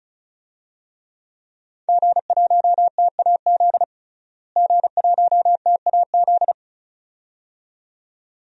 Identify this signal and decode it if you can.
Morse code, heard as a keyed tone, about 35 wpm, 700 Hz. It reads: G1TAZ G1TAZ
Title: DXCC-100 Call Signs - 5 Characters - Repeated 2x - 35wpm